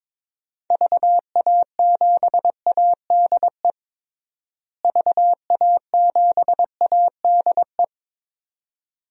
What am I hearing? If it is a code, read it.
Morse code, heard as a keyed tone, about 21 wpm, 700 Hz. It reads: VA7ADE VA7ADE
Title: DXCC-100 Call Signs - 6 Characters - Repeated 2x - 22wpm